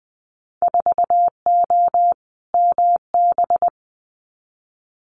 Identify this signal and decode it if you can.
Morse code, heard as a keyed tone, about 20 wpm, 700 Hz. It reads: 4O MB
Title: Winter Field Day Exchanges Encore 20wpm